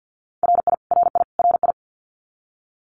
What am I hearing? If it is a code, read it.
Morse code, heard as a keyed tone, about 49 wpm, 700 Hz. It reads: RI RI RI
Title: US States Repeated 3x 50wpm